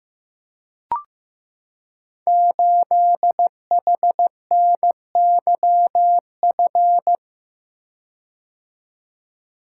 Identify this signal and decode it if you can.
Morse code, heard as a keyed tone, about 15 wpm, 700 Hz. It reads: 8HNYF
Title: Code Groups - 5 Letter:Numbers - 15wpm